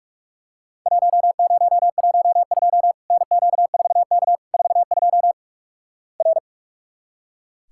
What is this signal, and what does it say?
Morse code, heard as a keyed tone, about 45 wpm, 700 Hz. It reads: 1012 NQ4K 42 R
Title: Worked All Europe DX Contest - QTC Exchange - 45wpm